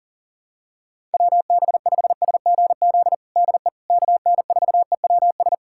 Morse code, heard as a keyed tone, about 40 wpm, 700 Hz. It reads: WB5SGZ DE KN4EWS
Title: USA Callsign DE Callsign 40wpm